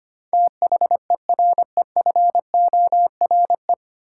Morse code, heard as a keyed tone, about 25 wpm, 700 Hz. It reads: THEREFORE
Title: Sentences from Top 800 Words - New Word Drill - 25wpm